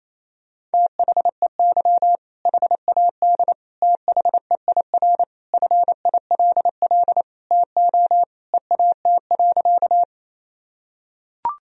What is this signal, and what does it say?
Morse code, heard as a keyed tone, about 28 wpm, 700 Hz. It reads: THEY HAD THEIR FILL TO EAT.
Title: Sentences from Top 1000 Words - Review Part 3 - 28wpm